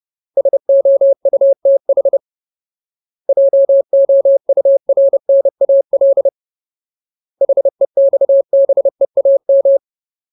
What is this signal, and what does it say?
Morse code, heard as a keyed tone, about 30 wpm, 550 Hz. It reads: SOUTH JOURNAL HEXBEAM
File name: Ultimate Word List / 30WPM Ultimate